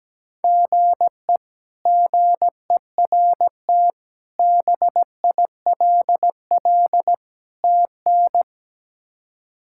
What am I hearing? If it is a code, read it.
Morse code, heard as a keyed tone, about 17 wpm, 700 Hz. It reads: GE GERT BILL TN
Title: K1USN SST QSO Practice - 17wpm